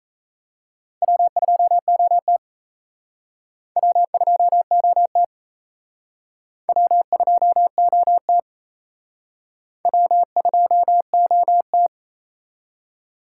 Morse code, heard as a keyed tone, about 34 wpm, 700 Hz. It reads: W2OT W2OT W2OT W2OT
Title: Speed Racing - USA 1x2 Call Signs 28wpm